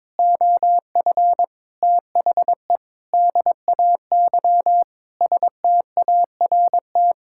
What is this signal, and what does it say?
Morse code, heard as a keyed tone, about 22 wpm, 700 Hz. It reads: OF THE DAY START
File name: Sentences with 7-Word Phrases - 22wpm